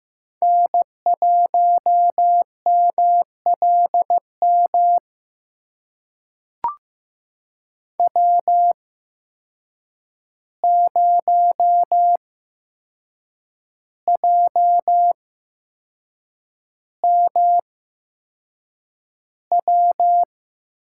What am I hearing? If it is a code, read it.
Morse code, heard as a keyed tone, about 16 wpm, spaced as for 6 wpm, 700 Hz. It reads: N1MLM W0JMW
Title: USA Call Signs - 5 Characters - 8x Character Spacing - 15wpm